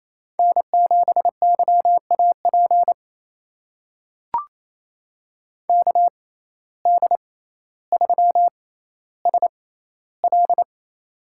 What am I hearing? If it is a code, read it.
Morse code, heard as a keyed tone, about 28 wpm, spaced as for 14 wpm, 700 Hz. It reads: N7YAP KD3SL
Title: USA Call Signs - 5 Characters - 6x Character Spacing - 28wpm